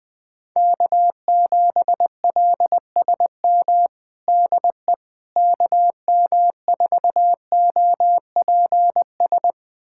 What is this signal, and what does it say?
Morse code, heard as a keyed tone, about 20 wpm, 700 Hz. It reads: K7LSM DE KM4OPS
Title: USA Callsign DE Callsign 20wpm